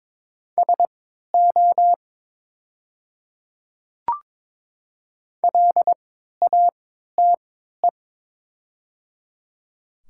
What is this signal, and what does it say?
Morse code, heard as a keyed tone, about 22 wpm, spaced as for 13 wpm, 700 Hz. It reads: SO LATE
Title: 3x Character Spacing - Top 300 Words - 22wpm